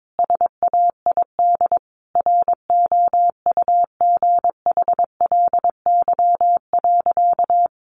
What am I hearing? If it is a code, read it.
Morse code, heard as a keyed tone, about 22 wpm, 700 Hz. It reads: SAID ROUGHLY.
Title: Sentences from Top 1500 Words 22wpm